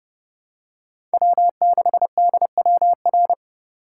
Morse code, heard as a keyed tone, about 30 wpm, 700 Hz. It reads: W6DWR
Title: USA Call Signs - 5 Characters - 3x Character Spacing - 30wpm